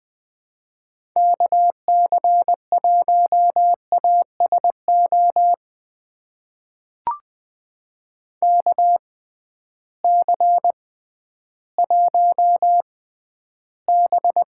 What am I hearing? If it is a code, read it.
Morse code, heard as a keyed tone, about 20 wpm, spaced as for 12 wpm, 700 Hz. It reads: KC1ASO KC1B
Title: USA Call Signs - 6 Characters - 6x Character Spacing - 20wpm